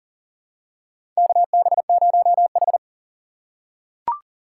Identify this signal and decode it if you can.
Morse code, heard as a keyed tone, about 40 wpm, 700 Hz. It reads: KB0H
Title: USA Call Signs - 4 Characters - 4x Character Spacing - 40wpm